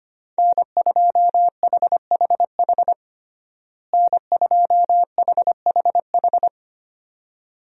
Morse code, heard as a keyed tone, about 25 wpm, 700 Hz. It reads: N2HHH N2HHH
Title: USA Call Signs - 5 Characters - Repeated 2x - 25wpm